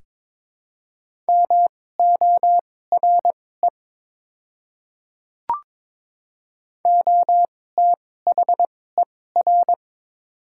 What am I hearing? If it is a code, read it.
Morse code, heard as a keyed tone, about 22 wpm, spaced as for 16 wpm, 700 Hz. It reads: MORE OTHER
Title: 2x Character Spacing - Top 100 Words - 22wpm